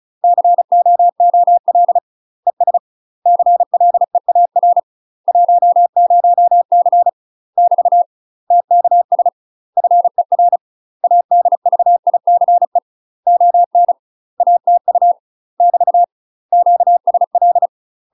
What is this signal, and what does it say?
Morse code, heard as a keyed tone, about 36 wpm, 700 Hz. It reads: COOL ES CLEAR 10C <BT> TKS FER ADVICE ON ATU <BT> QSL
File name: UK MASTER QSOs TRAINING / 35WPM / UK MASTER QSOs 0010